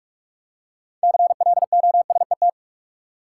Morse code, heard as a keyed tone, about 44 wpm, 700 Hz. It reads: CLOSET
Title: Six Letter Words - 45wpm